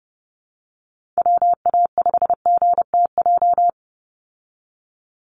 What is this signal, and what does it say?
Morse code, heard as a keyed tone, about 30 wpm, 700 Hz. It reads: WA5GTJ
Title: USA Call Signs 30wpm